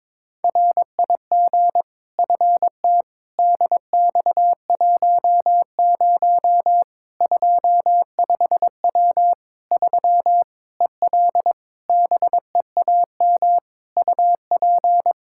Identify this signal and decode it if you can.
Morse code, heard as a keyed tone, about 22 wpm, 700 Hz. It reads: RIG FT DX10 25W 3 EL BEAM UP
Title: Practice QSO - Exchange 2 - Info - 22wpm